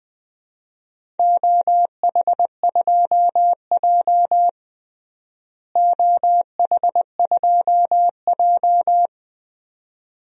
Morse code, heard as a keyed tone, about 20 wpm, 700 Hz. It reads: OH2J OH2J
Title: DXCC-100 Call Signs - 4 Characters - Repeated 2x - 20wpm